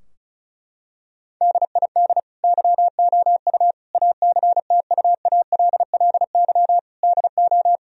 Morse code, heard as a keyed tone, about 36 wpm, 700 Hz. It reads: DID YOU ACTUALLY DO
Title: Sentences from Top 1000 Words - Review Part 4 - 35wpm